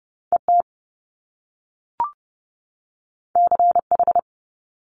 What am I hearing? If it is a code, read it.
Morse code, heard as a keyed tone, about 30 wpm, 700 Hz. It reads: ET CH
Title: Top Two Letter Combinations in Words 30wpm